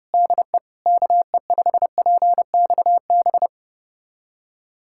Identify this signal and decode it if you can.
Morse code, heard as a keyed tone, about 30 wpm, 690 Hz. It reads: DE KE5PXB
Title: USA Callsign DE Callsign 30wpm